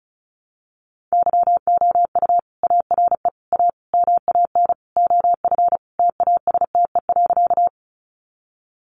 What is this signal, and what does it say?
Morse code, heard as a keyed tone, about 34 wpm, 700 Hz. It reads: YOU ARE A MAN OF TASTE.
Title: Sentences from Top 1100 Words 35wpm